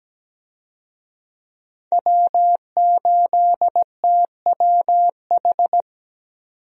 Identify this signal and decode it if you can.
Morse code, heard as a keyed tone, about 17 wpm, 700 Hz. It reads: W8TWH
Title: USA Call Signs - 5 Characters - 17wpm